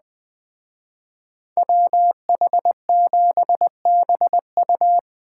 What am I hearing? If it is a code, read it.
Morse code, heard as a keyed tone, about 20 wpm, 700 Hz. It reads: WH7BU
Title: Speed Racing - USA Call Signs 17wpm